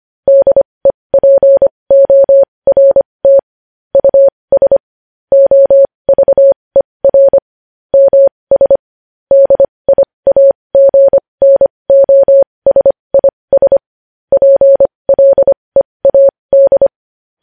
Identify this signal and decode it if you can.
Morse code, heard as a keyed tone, about 25 wpm, spaced as for 20 wpm, 550 Hz. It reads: DEPORT US OVER MS DIAGNOSIS PLEAD
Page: news-headlines-in-morse-code-at-20-wpm